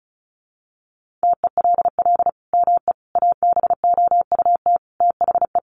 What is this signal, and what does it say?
Morse code, heard as a keyed tone, about 34 wpm, 700 Hz. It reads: TELL ME ABOUT THE
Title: Sentences from Top 1500 Words 35wpm